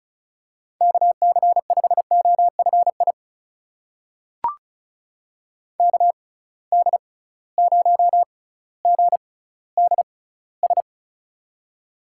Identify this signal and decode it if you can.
Morse code, heard as a keyed tone, about 36 wpm, spaced as for 18 wpm, 700 Hz. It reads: KC5OFI KD0GDS
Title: USA Call Signs - 6 Characters - 6x Character Spacing - 35wpm